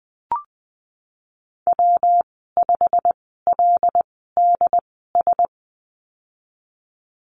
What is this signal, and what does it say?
Morse code, heard as a keyed tone, about 19 wpm, spaced as for 14 wpm, 700 Hz. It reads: W5LDS
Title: 128 Lesson 21g - Number 6 - Call Signs 20wpm